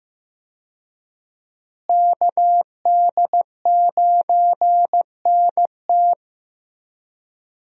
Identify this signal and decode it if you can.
Morse code, heard as a keyed tone, about 16 wpm, 700 Hz. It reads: KD9NT
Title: USA Call Signs - 5 Characters - Repeated 2x - 15wpm